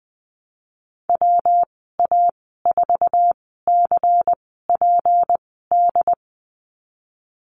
Morse code, heard as a keyed tone, about 19 wpm, spaced as for 14 wpm, 700 Hz. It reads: WA4CPD
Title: 219 Lesson 35g - Number 0 - Call Signs 20wpm